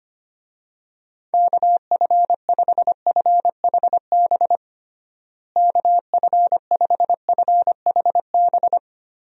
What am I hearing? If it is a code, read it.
Morse code, heard as a keyed tone, about 25 wpm, 700 Hz. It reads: KF5FHB KF5FHB
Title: USA Call Signs - 6 Characters - Repeated 3x - 25wpm